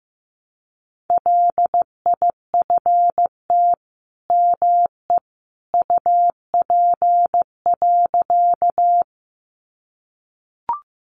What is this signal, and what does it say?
Morse code, heard as a keyed tone, about 15 wpm, 700 Hz. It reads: LIFT ME UP.
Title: Sentences from Top 1000 Words 15wpm